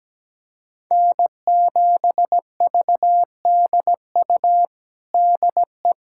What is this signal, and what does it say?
Morse code, heard as a keyed tone, about 17 wpm, 700 Hz. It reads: N7VDU DE
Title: USA Callsign DE Callsign - 5 Characters - 17wpm